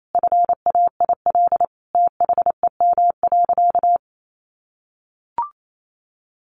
Morse code, heard as a keyed tone, about 28 wpm, 700 Hz. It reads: FAIL THEM.
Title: Sentences from Top 1100 Words 28wpm